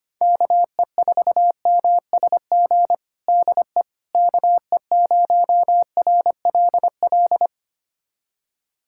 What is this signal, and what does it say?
Morse code, heard as a keyed tone, about 25 wpm, 700 Hz. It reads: KE4MSG DE KE0RLL
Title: USA Callsign DE Callsign - 6 Characters - 25wpm